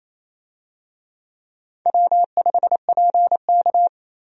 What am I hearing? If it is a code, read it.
Morse code, heard as a keyed tone, about 28 wpm, 700 Hz. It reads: W5PK
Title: USA Call Signs - 4 Characters - Repeated 3x - 28wpm